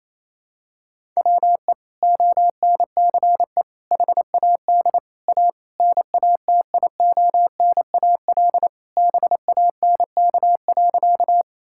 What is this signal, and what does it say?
Morse code, heard as a keyed tone, about 28 wpm, 700 Hz. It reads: WE ONCE HAD A NATIONAL BANK.
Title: Sentences from Top 1000 Words - Review Part 6 - 28wpm